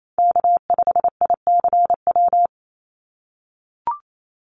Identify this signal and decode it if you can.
Morse code, heard as a keyed tone, about 28 wpm, 700 Hz. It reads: K5ICW
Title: USA Call Signs 28wpm